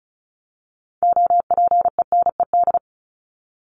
Morse code, heard as a keyed tone, about 34 wpm, 700 Hz. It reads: OPENED
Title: ED Suffix - Top 100 Words with Base Words 35wpm